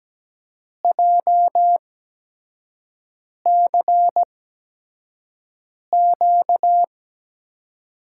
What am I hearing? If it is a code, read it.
Morse code, heard as a keyed tone, about 17 wpm, spaced as for 5 wpm, 700 Hz. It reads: JCQ
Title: USA Call Signs - 6 Characters - 8x Character Spacing - 17wpm